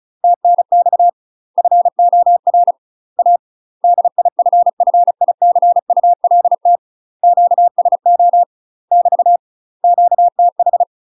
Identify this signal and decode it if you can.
Morse code, heard as a keyed tone, about 36 wpm, 700 Hz. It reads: TNX FOR A DIFFICULT QSO <BT> QTH